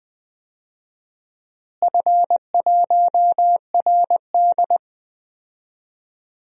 Morse code, heard as a keyed tone, about 21 wpm, 700 Hz. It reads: F1RD
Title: DXCC-100 Call Signs - 4 Characters - Repeated 3x - 20wpm